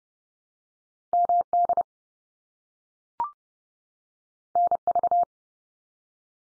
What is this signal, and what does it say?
Morse code, heard as a keyed tone, about 30 wpm, 700 Hz. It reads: MD NV
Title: US States 30WPM